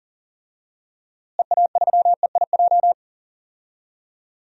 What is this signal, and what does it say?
Morse code, heard as a keyed tone, about 40 wpm, 700 Hz. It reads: EA3EIJ
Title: DXCC-100 Call Signs - 6 Characters - Repeated 3x - 40wpm